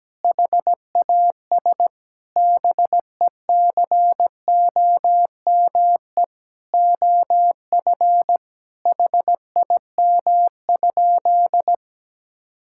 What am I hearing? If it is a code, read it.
Morse code, heard as a keyed tone, about 17 wpm, 700 Hz. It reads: HAS BECOME OF HIM?
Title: Sentences from Top 1000 Words - Review Part 5 - 17wpm